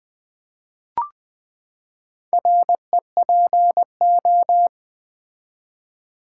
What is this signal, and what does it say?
Morse code, heard as a keyed tone, about 20 wpm, 700 Hz. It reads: REPO
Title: Build A Word - 4 Letters At A Time - 8-Letter Words - 20wpm